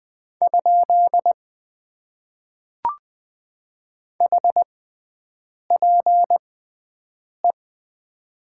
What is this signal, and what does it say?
Morse code, heard as a keyed tone, about 20 wpm, spaced as for 7 wpm, 700 Hz. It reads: ? HPE
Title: Top 90 QSO Elements - 6x Character Spacing - 20wpm